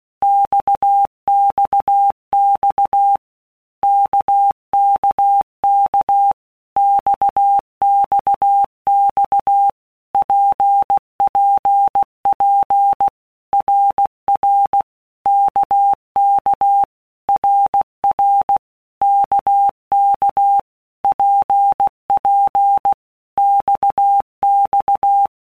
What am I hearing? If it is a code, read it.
Morse code, heard as a keyed tone, about 16 wpm, 800 Hz. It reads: XXX KKK XXX PPP RR KK RR KK PP XX
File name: Basic Letter Groups By Blocks / Groups@15 / RKPX@15